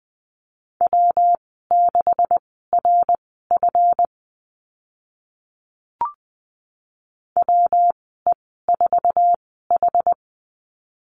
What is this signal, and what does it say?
Morse code, heard as a keyed tone, about 20 wpm, spaced as for 14 wpm, 700 Hz. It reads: W6RF WE4H
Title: 140 Lesson 23g - Letter f - Call Signs 20wpm